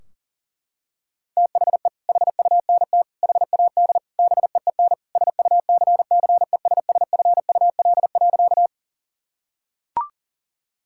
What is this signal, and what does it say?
Morse code, heard as a keyed tone, about 40 wpm, 700 Hz. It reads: THE HUNT HAD BEEN SUCCESSFUL.
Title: Sentences from Top 1600 Words 40wpm